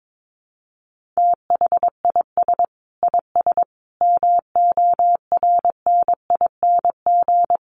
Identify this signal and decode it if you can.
Morse code, heard as a keyed tone, about 22 wpm, 700 Hz. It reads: THIS IS MORNING
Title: Sets of 5 Words Encore2 22wpm